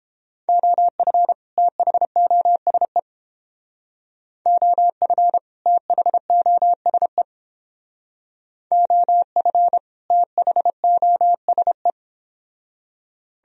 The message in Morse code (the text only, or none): OF THOSE OF THOSE OF THOSE